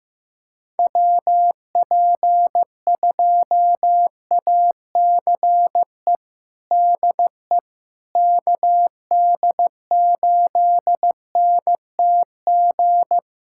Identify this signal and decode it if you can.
Morse code, heard as a keyed tone, about 16 wpm, 700 Hz. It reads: WP2ACE DE KD8NTG
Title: USA Callsign DE Callsign - 6 Characters - 15wpm